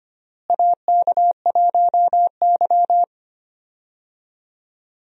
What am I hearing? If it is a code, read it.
Morse code, heard as a keyed tone, about 25 wpm, 700 Hz. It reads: AK1Y
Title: USA Call Signs - 4 Characters - Repeated 3x - 25wpm